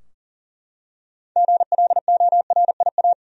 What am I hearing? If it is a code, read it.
Morse code, heard as a keyed tone, about 40 wpm, 700 Hz. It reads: GLORIA
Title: USA Common Names - 40wpm